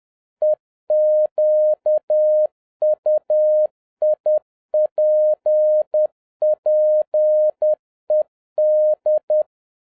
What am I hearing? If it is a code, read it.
Morse code, heard as a keyed tone, about 10 wpm, 610 Hz. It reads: EQUIPPED
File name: 2011-10-mp3 / n11-MEETING WITH GOOD FRIENDS 10wpm